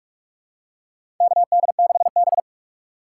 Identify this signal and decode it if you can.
Morse code, heard as a keyed tone, about 46 wpm, 700 Hz. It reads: KD6B
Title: USA Call Signs - 4 Characters - 4x Character Spacing - 45wpm